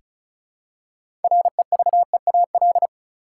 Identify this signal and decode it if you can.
Morse code, heard as a keyed tone, about 34 wpm, 700 Hz. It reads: REVEAL
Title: Six Letter Words - 35wpm